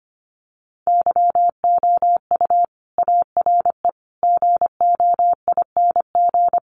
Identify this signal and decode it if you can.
Morse code, heard as a keyed tone, about 25 wpm, 700 Hz. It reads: YOU ARE GOING
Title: Sets of 4 Words 25wpm